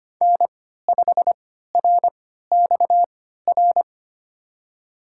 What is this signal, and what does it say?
Morse code, heard as a keyed tone, about 25 wpm, spaced as for 14 wpm, 700 Hz. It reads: N5RXR
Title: USA Call Signs - 5 Characters - 3x Character Spacing - 25wpm